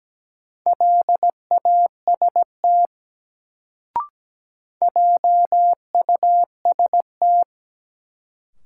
Morse code, heard as a keyed tone, about 17 wpm, 700 Hz. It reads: LAST JUST
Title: Sets of 2 Words - New Word Drill - 17wpm